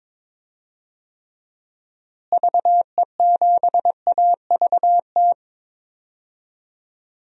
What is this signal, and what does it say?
Morse code, heard as a keyed tone, about 22 wpm, 700 Hz. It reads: VE7AVT
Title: DXCC-100 Call Signs - 6 Characters - Repeated 3x - 22wpm